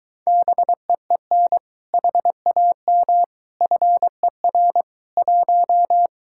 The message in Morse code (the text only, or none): BEEN HAM FER 1